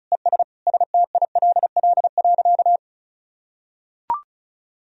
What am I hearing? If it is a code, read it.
Morse code, heard as a keyed tone, about 36 wpm, 700 Hz. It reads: ES STILL.
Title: Sentences from Top 400 Words and Abbreviations 35wpm